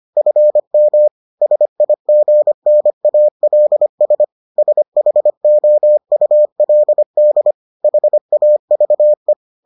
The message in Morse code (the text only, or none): FM SIGNALS SHOULD HAVE